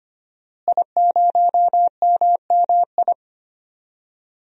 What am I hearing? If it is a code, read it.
Morse code, heard as a keyed tone, about 25 wpm, 700 Hz. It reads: I0MMI
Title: DXCC-100 Call Signs - 5 Characters - Repeated 3x - 25wpm